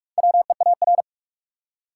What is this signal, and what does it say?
Morse code, heard as a keyed tone, about 43 wpm, 700 Hz. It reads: WEAR
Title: Four Letter Words - 45wpm